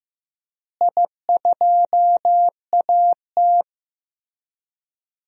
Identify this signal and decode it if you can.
Morse code, heard as a keyed tone, about 15 wpm, 700 Hz. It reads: I2AT
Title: DXCC-100 Call Signs - 4 Characters - Repeated 2x - 15wpm